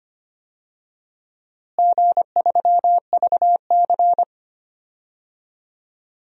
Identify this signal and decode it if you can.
Morse code, heard as a keyed tone, about 25 wpm, 700 Hz. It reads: G3VC
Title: DXCC-100 Call Signs - 4 Characters - Repeated 2x - 25wpm